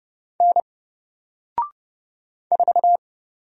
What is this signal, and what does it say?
Morse code, heard as a keyed tone, about 30 wpm, 700 Hz. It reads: N 4
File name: Single-Letter-Number-30wpm